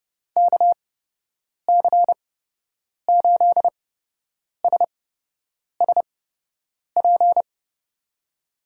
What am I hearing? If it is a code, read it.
Morse code, heard as a keyed tone, about 30 wpm, spaced as for 8 wpm, 700 Hz. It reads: KC8SSP
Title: USA Call Signs - 8x Character Spacing - 30wpm